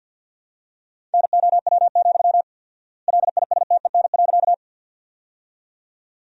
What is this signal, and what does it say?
Morse code, heard as a keyed tone, about 53 wpm, 700 Hz. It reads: NOW, LISTEN.
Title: 4x Word-Spacing - Sentences from Top 400 Words - 50wpm